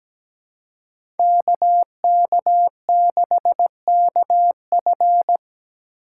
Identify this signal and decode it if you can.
Morse code, heard as a keyed tone, about 17 wpm, 700 Hz. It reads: KK6KF
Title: USA Call Signs - 5 Characters - 2x Character Spacing - 17wpm